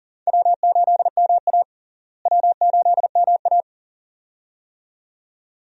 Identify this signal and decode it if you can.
Morse code, heard as a keyed tone, about 40 wpm, 700 Hz. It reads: W8MA W8MA
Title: USA Call Signs - 4 Characters - Repeated 2x - 40wpm